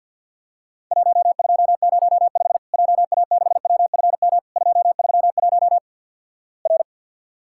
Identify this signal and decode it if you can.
Morse code, heard as a keyed tone, about 49 wpm, 700 Hz. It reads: 1205 JA6WFM 231 R
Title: Worked All Europe DX Contest - QTC Exchange with Repeats - 50wpm